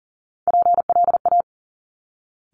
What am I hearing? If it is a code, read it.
Morse code, heard as a keyed tone, about 38 wpm, 700 Hz. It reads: PLA
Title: Top Three Letter Combinations in Words 40wpm